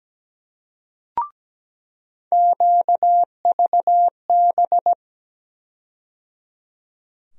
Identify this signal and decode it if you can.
Morse code, heard as a keyed tone, about 17 wpm, 700 Hz. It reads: QVB
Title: USA Call Sign Postfixes - 3 Letters 17wpm